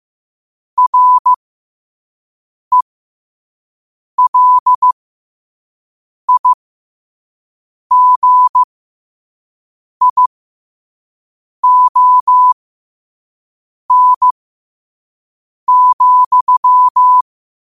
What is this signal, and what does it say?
Morse code, heard as a keyed tone, about 15 wpm, spaced as for 5 wpm, 1000 Hz. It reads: RELIGION,
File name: qotdfiles / 5wpm